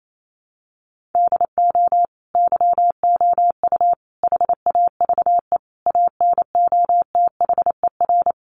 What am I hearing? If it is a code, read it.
Morse code, heard as a keyed tone, about 28 wpm, 700 Hz. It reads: DO YOU HAVE ANOTHER
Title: Sentences from Top 600 Words 28wpm